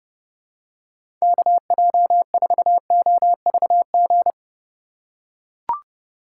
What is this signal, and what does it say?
Morse code, heard as a keyed tone, about 30 wpm, 700 Hz. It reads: KJ4OVG
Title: USA Call Signs - 6 Characters - 6x Character Spacing - 30wpm